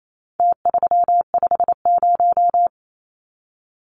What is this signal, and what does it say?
Morse code, heard as a keyed tone, about 28 wpm, 700 Hz. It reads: T350
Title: Numbers - Contest Style - 4-Digit 28wpm